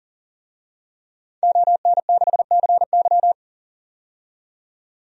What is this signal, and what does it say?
Morse code, heard as a keyed tone, about 40 wpm, 700 Hz. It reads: ON6CY
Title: DXCC-100 Call Signs - 5 Characters - 40wpm